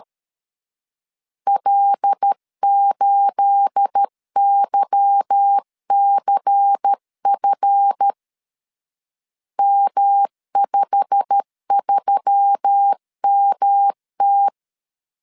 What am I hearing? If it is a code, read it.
Morse code, heard as a keyed tone, about 13 wpm, 800 Hz. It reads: L8YCF M53MT
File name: LEZIONI / 9 28 12wpm 2351